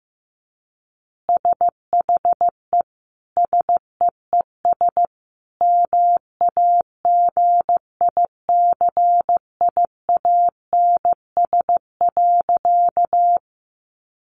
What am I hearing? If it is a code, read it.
Morse code, heard as a keyed tone, about 15 wpm, 700 Hz. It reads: SHE SEES MAGICIANS.
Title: Easy Phrases 15wpm